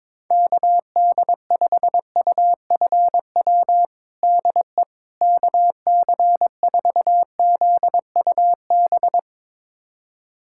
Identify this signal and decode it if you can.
Morse code, heard as a keyed tone, about 22 wpm, 700 Hz. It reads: KD5UFW DE KC4ZUB
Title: USA Callsign DE Callsign - 6 Characters - 22wpm